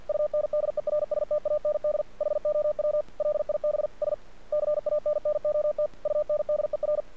Morse code, heard as a keyed tone, about 50 wpm, 600 Hz. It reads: UNDERSTAND HOW LIBS CANNOT UNDER
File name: archived / CRSnet